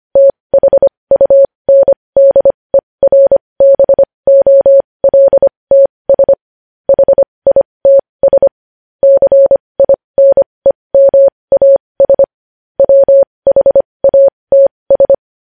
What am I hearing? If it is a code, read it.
Morse code, heard as a keyed tone, about 25 wpm, spaced as for 20 wpm, 550 Hz. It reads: THUNDERBOLTS HITS CINEMAS WHATS
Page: news headlines in morse code at 20 wpm